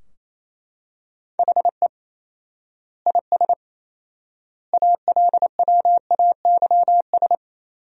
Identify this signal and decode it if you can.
Morse code, heard as a keyed tone, about 28 wpm, 700 Hz. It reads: HE IS ALWAYS
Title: 4x Word-Spacing - Sentences from Top 300 Words - 28wpm